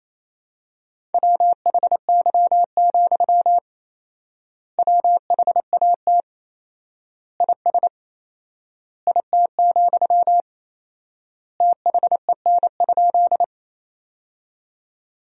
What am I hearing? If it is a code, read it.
Morse code, heard as a keyed tone, about 28 wpm, 700 Hz. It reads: WHY, WHAT IS IT, THEN?
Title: 4x Word-Spacing - Sentences from Top 200 Words - 28wpm